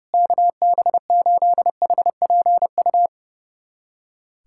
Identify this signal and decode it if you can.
Morse code, heard as a keyed tone, about 32 wpm, 700 Hz. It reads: KB8HPU
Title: USA Callsign DE Callsign - 6 Characters - 30wpm